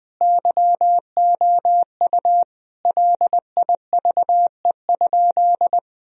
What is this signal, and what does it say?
Morse code, heard as a keyed tone, about 20 wpm, 700 Hz. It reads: YOU LIVE?
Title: Sentences from Top 1000 Words - Review Part 4 - 20wpm